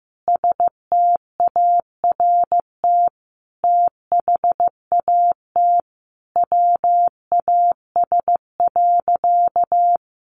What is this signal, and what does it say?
Morse code, heard as a keyed tone, about 14 wpm, 700 Hz. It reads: START THAT WAS.
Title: Sentences from Top 300 Words 15wpm